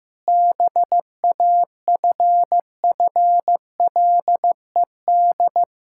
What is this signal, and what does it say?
Morse code, heard as a keyed tone, about 16 wpm, 700 Hz. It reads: BAFFLED
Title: Pangram Challenge v2 - 15wpm